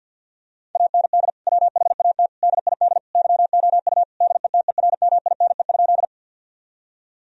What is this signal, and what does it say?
Morse code, heard as a keyed tone, about 50 wpm, 700 Hz. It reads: AND WHAT DID YOU DETERMINE?
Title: Sentences from Top 1000 Words - Review Part 4 - 50wpm